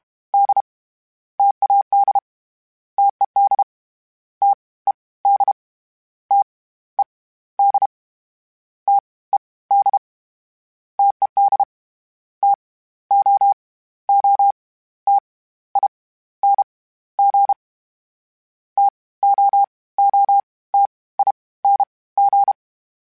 Morse code, heard as a keyed tone, about 32 wpm, spaced as for 17 wpm, 800 Hz. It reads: D TAD TED TED TED TED TED TOOTING TOOTING
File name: Difficult Combos / Difficult Combos-mp3s / T AOEI @30